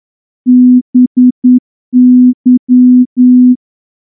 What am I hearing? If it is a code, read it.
Morse code, heard as a keyed tone, about 9 wpm, 250 Hz. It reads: BY
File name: morse-training-10WPM